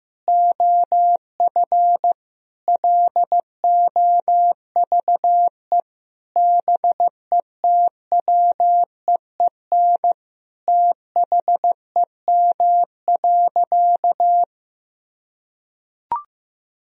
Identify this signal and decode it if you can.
Morse code, heard as a keyed tone, about 15 wpm, 700 Hz. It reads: OF LOVE BETWEEN THEM.
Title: Sentences from Top 1000 Words - Review Part 1 - 15wpm